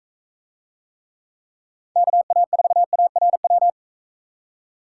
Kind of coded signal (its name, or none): Morse code